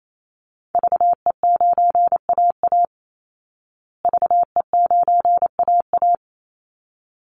Morse code, heard as a keyed tone, about 28 wpm, 700 Hz. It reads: VE9AA VE9AA
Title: Call Signs Repeated 3x - 28wpm